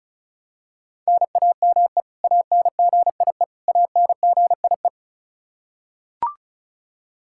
Morse code, heard as a keyed tone, about 34 wpm, 700 Hz. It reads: NAME ANGIE ANGIE
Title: Practice QSO - Exchange 1 - Beginning - 35wpm